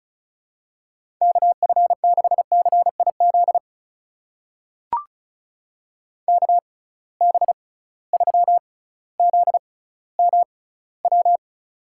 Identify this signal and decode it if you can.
Morse code, heard as a keyed tone, about 34 wpm, spaced as for 18 wpm, 700 Hz. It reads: KF6CIZ KB3ZMW
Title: USA Call Signs - 6 Characters - 6x Character Spacing - 35wpm